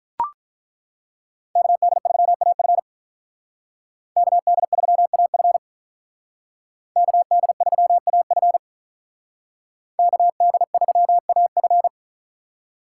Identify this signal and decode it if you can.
Morse code, heard as a keyed tone, about 42 wpm, 700 Hz. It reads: KD3AF KD3AF KD3AF KD3AF
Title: Speed Racing - USA 2x2 Call Signs 35wpm